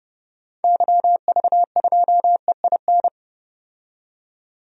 Morse code, heard as a keyed tone, about 30 wpm, 700 Hz. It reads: YV2EIN
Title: DXCC-100 Call Signs - 6 Characters - 30wpm